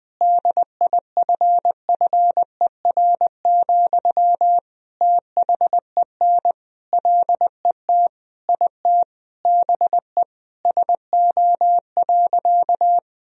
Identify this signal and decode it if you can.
Morse code, heard as a keyed tone, about 20 wpm, 700 Hz. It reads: DIFFER, THEN LET IT BE SO.
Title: Sentences from Top 1000 Words - Review Part 1 - 20wpm